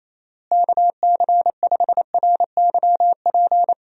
Morse code, heard as a keyed tone, about 28 wpm, 700 Hz. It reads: KC5RYP
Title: USA Call Signs - 6 Characters - Repeated 2x - 28wpm